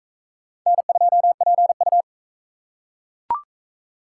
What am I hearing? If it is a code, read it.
Morse code, heard as a keyed tone, about 41 wpm, 700 Hz. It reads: N2PU